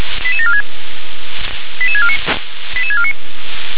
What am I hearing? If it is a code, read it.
Five-tone selective call (ZVEI), heard as quick runs of five tones, tones of about 70 ms. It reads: 88746 88649 88649